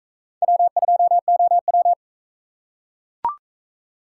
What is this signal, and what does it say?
Morse code, heard as a keyed tone, about 40 wpm, 700 Hz. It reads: W2OW